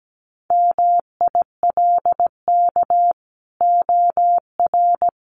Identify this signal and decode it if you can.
Morse code, heard as a keyed tone, about 17 wpm, 700 Hz. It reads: MILK OR
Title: Sentences from Top 1500 Words 17wpm